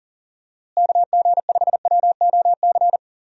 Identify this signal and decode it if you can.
Morse code, heard as a keyed tone, about 37 wpm, 700 Hz. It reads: KG5WOC
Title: USA Call Signs - 6 Characters - Repeated 3x - 40wpm